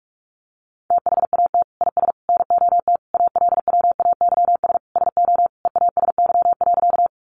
Morse code, heard as a keyed tone, about 44 wpm, 700 Hz. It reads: THAT IS NOT ALWAYS SO EASY.
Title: Sentences from Top 700 Words 45wpm